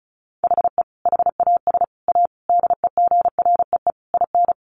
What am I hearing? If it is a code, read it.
Morse code, heard as a keyed tone, about 34 wpm, 700 Hz. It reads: HE HAS A DEGREE IN
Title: Sentences from Top 1500 Words 35wpm